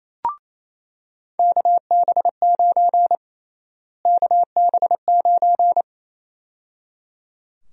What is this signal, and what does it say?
Morse code, heard as a keyed tone, about 28 wpm, 700 Hz. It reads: KB9 KB9
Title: USA Call Sign Prefixes - Repeated 2x - 28wpm